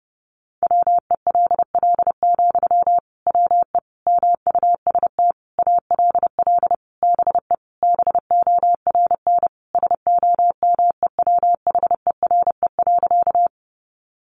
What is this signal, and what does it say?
Morse code, heard as a keyed tone, about 30 wpm, 700 Hz. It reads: WELL, WE MUST ALL BE BORN SOMEWHERE.
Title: Sentences from Top 1400 Words 30wpm